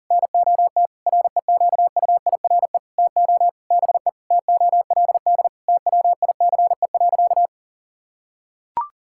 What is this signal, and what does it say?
Morse code, heard as a keyed tone, about 40 wpm, 700 Hz. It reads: NOT REQUIRE TO BE TOLD TWICE.
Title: Sentences from Top 1600 Words 40wpm